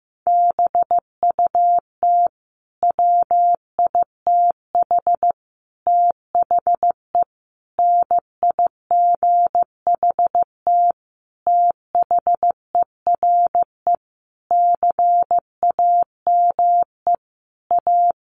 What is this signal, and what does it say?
Morse code, heard as a keyed tone, about 14 wpm, 700 Hz. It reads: BUT WITH THE NIGHT THERE CAME A
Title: Sentences from Top 1500 Words 15wpm